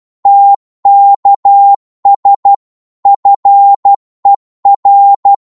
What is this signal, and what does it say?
Morse code, heard as a keyed tone, about 12 wpm, 800 Hz. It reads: TKS FER
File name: Morse / qso-12wpm